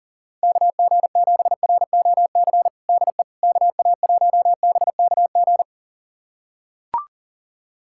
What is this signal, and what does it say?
Morse code, heard as a keyed tone, about 40 wpm, 700 Hz. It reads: KG7ROC DE KA1BKG
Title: USA Callsign DE Callsign - 6 Characters - 40wpm